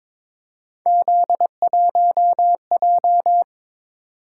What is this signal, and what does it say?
Morse code, heard as a keyed tone, about 23 wpm, 700 Hz. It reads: Z1J
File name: practice / 22wpm / Three Letter-Number Combinations - 22wpm